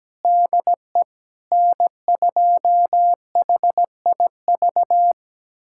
Morse code, heard as a keyed tone, about 17 wpm, 700 Hz. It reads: DE N2HIV
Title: USA Callsign DE Callsign - 5 Characters - 17wpm